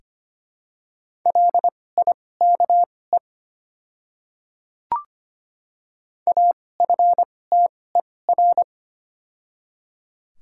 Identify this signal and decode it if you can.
Morse code, heard as a keyed tone, about 25 wpm, spaced as for 18 wpm, 700 Hz. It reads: LIKE AFTER
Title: 2x Character Spacing - Top 300 Words - 25wpm